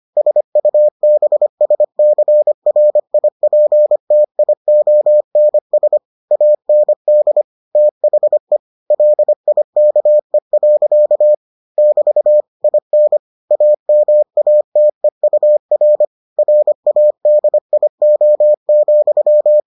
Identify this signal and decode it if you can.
Morse code, heard as a keyed tone, about 25 wpm, 600 Hz. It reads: SUBSCRIPTIONS AND THE LIKE. <BT> IN AMATEUR RADIO,